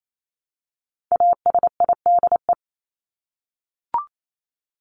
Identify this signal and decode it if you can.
Morse code, heard as a keyed tone, about 28 wpm, 700 Hz. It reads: ASIDE
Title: Top 1751-2000 Words 28wpm